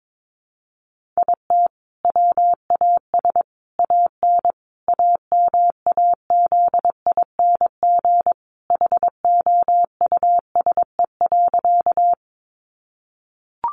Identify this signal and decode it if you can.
Morse code, heard as a keyed tone, about 22 wpm, 700 Hz. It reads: IT WAS AN AMAZING HOUSE.
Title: Sentences from Top 1500 Words 22wpm